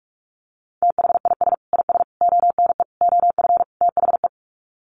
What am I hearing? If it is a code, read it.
Morse code, heard as a keyed tone, about 44 wpm, 700 Hz. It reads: THIS IS ONE OF THE
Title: Sets of 5 Words 45wpm